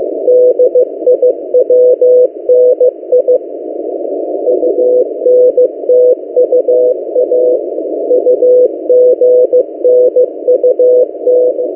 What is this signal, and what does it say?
Morse code, heard as a keyed tone, about 15 wpm, 500 Hz. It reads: DIWNI UNTUA UGNUT
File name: M08a- -1355utc-8096khz